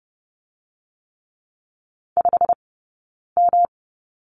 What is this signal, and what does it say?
Morse code, heard as a keyed tone, about 30 wpm, 700 Hz. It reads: 5 M
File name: Lesson 20d - Number 3 - Two Characters 30wpm